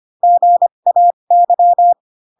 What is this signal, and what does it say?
Morse code, heard as a keyed tone, about 25 wpm, spaced as for 22 wpm, 700 Hz. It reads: GAY